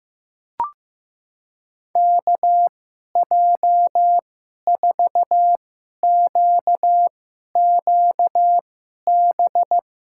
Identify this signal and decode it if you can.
Morse code, heard as a keyed tone, about 15 wpm, spaced as for 11 wpm, 700 Hz. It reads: KJ4QQB
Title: USA Call Signs - 6 Characters - 2x Character Spacing - 15wpm